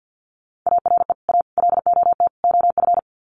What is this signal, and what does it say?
Morse code, heard as a keyed tone, about 50 wpm, 700 Hz. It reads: ARE A LOT OF
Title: Sets of 4 Words 50wpm